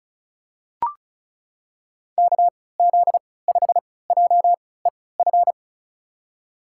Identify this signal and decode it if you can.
Morse code, heard as a keyed tone, about 36 wpm, spaced as for 20 wpm, 700 Hz. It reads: KZ5JEF
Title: USA Call Signs - 6 Characters - 3x Character Spacing - 35wpm